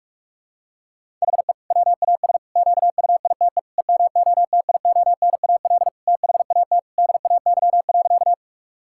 Morse code, heard as a keyed tone, about 46 wpm, 700 Hz. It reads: HE WAS QUITE EMOTIONAL THAT DAY.
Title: Sentences from Top 1000 Words - Review Part 5 - 45wpm